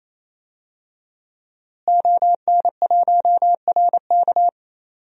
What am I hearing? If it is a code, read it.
Morse code, heard as a keyed tone, about 28 wpm, 700 Hz. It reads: ON1RK
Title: DXCC-100 Call Signs - 5 Characters - Repeated 3x - 28wpm